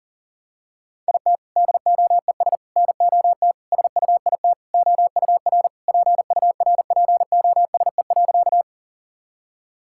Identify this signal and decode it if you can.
Morse code, heard as a keyed tone, about 37 wpm, 700 Hz. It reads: IT DOES NOT SUIT OUR PURPOSE.
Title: Sentences from Top 1600 Words 40wpm